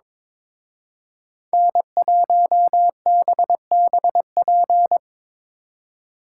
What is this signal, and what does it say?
Morse code, heard as a keyed tone, about 25 wpm, 700 Hz. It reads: N1BBP
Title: Speed Racing - USA 1x3 Call Signs 22wpm